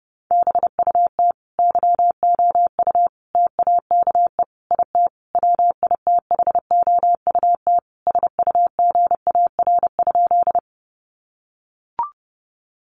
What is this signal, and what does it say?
Morse code, heard as a keyed tone, about 30 wpm, 700 Hz. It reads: BUT YOU TAKE IT WITHOUT SUGAR?
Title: Sentences from Top 1200 Words 30wpm